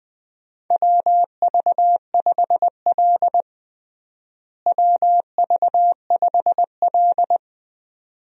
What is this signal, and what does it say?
Morse code, heard as a keyed tone, about 20 wpm, 700 Hz. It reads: WV5L WV5L
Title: USA Call Signs - 4 Characters - Repeated 2x - 20wpm